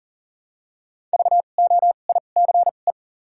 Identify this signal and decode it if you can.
Morse code, heard as a keyed tone, about 40 wpm, 700 Hz. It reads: V O I C E
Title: 2x Character Spacing - Top 500 Words - 40wpm